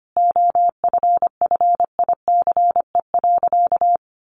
Morse code, heard as a keyed tone, about 25 wpm, 700 Hz. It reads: OFFICE.
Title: Sentences from Top 1000 Words 25wpm